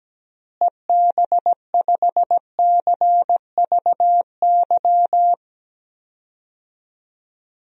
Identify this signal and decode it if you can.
Morse code, heard as a keyed tone, about 17 wpm, 700 Hz. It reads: EB5CVY
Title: DXCC-100 Call Signs - 6 Characters - Repeated 2x - 17wpm